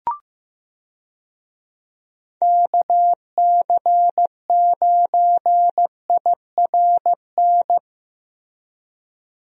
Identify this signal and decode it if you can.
Morse code, heard as a keyed tone, about 15 wpm, 700 Hz. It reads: KC9IRN
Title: USA Call Signs - 6 Characters - Repeated 3x - 15wpm